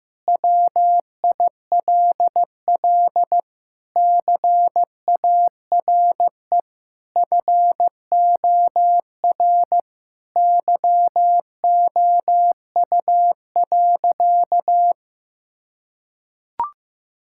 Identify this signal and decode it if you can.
Morse code, heard as a keyed tone, about 15 wpm, 700 Hz. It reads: WILL CARE FOR YOU.
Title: Sentences from Top 1000 Words - Review Part 6 - 15wpm